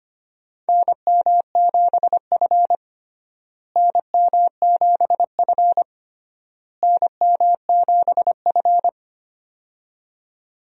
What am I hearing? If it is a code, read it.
Morse code, heard as a keyed tone, about 25 wpm, 700 Hz. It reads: NM7F NM7F NM7F
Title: USA Call Signs - 4 Characters - Repeated 3x - 25wpm